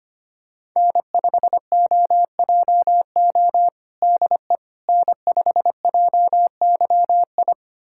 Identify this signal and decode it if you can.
Morse code, heard as a keyed tone, about 25 wpm, 700 Hz. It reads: N5OJO DE N5JYI
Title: USA Callsign DE Callsign - 5 Characters - 25wpm